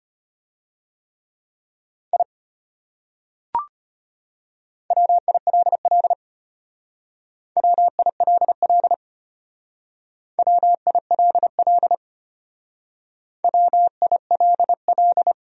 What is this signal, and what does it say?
Morse code, heard as a keyed tone, about 31 wpm, 700 Hz. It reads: I WILL WILL WILL WILL
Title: Speed-Racing Top 300 Words 25wpm